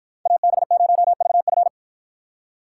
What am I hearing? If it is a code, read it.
Morse code, heard as a keyed tone, about 53 wpm, 700 Hz. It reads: AB0VF